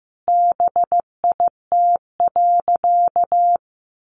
Morse code, heard as a keyed tone, about 15 wpm, 700 Hz. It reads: BIT.
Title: Sentences from Top 900 Words 15wpm